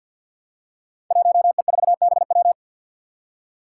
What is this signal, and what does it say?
Morse code, heard as a keyed tone, about 51 wpm, 700 Hz. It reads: 1E4BW
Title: Five Challenging Letter-Number Combinations 50wpm